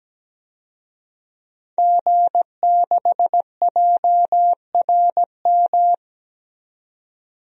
Morse code, heard as a keyed tone, about 17 wpm, 700 Hz. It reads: G6JRM
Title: DXCC-100 Call Signs - 5 Characters - 17wpm